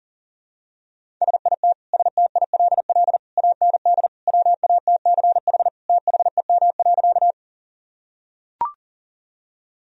Morse code, heard as a keyed tone, about 40 wpm, 700 Hz. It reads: SIT STILL AND WATCH THEM.
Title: Sentences from Top 1000 Words - Review Part 1 - 40wpm